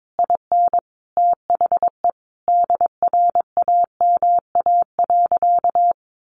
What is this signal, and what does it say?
Morse code, heard as a keyed tone, about 22 wpm, 700 Hz. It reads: IN THE DRAMA.
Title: Sentences from Top 1500 Words 22wpm